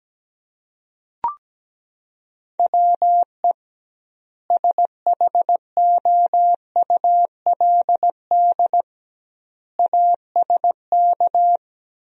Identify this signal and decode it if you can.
Morse code, heard as a keyed tone, about 17 wpm, 700 Hz. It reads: WE SHOULD ASK
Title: 2x Word-Spacing - Sentences from Top 300 Words - 17wpm